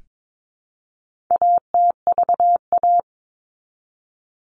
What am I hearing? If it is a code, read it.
Morse code, heard as a keyed tone, about 22 wpm, 700 Hz. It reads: ATVA
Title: All Cut Numbers - 4-Digit 22wpm